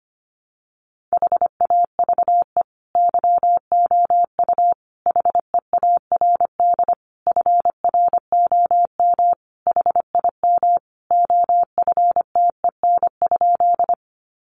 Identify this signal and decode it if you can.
Morse code, heard as a keyed tone, about 25 wpm, 700 Hz. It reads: HAVE YOU HEARD FROM HIM OFTEN?
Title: Sentences from Top 400 Words 25wpm